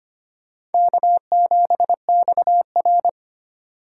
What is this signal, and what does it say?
Morse code, heard as a keyed tone, about 30 wpm, 700 Hz. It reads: K7XR
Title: Speed Racing - USA 1x2 Call Signs 25wpm